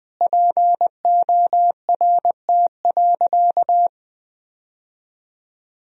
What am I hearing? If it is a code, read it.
Morse code, heard as a keyed tone, about 20 wpm, 700 Hz. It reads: PORT.
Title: Sentences from Top 1000 Words - Review Part 1 - 20wpm